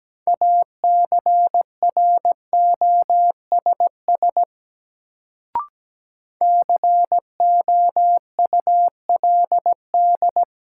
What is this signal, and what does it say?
Morse code, heard as a keyed tone, about 17 wpm, 700 Hz. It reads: ACROSS COULD
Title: Sets of 2 Words - New Word Drill - 17wpm